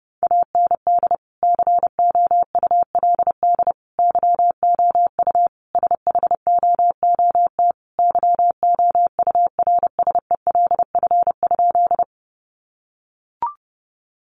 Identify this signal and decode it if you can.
Morse code, heard as a keyed tone, about 30 wpm, 700 Hz. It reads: AND COULD YOU SHOOT YOURSELF?
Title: Sentences from Top 1200 Words 30wpm